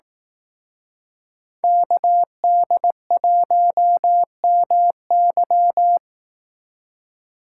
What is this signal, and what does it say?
Morse code, heard as a keyed tone, about 18 wpm, 700 Hz. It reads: KD1MY
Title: Speed Racing - USA 2x2 Call Signs 15wpm